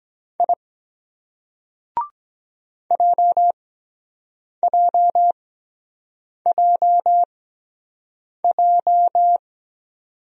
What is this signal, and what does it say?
Morse code, heard as a keyed tone, about 23 wpm, spaced as for 7 wpm, 700 Hz. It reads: I JJJJ